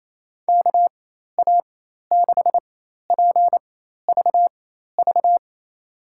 Morse code, heard as a keyed tone, about 27 wpm, spaced as for 13 wpm, 700 Hz. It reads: KA6PVV
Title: USA Call Signs - 6 Characters - 4x Character Spacing - 28wpm